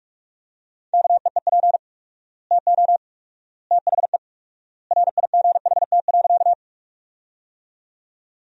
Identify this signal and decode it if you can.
Morse code, heard as a keyed tone, about 46 wpm, 700 Hz. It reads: KEEP TO THE RIGHT.
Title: 4x Word-Spacing - Sentences from Top 300 Words - 45wpm